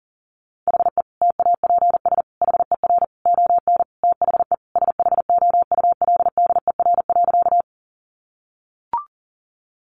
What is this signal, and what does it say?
Morse code, heard as a keyed tone, about 40 wpm, 700 Hz. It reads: HE TAPS HER ON THE SHOULDER.
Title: Additional Easy Phrases 40wpm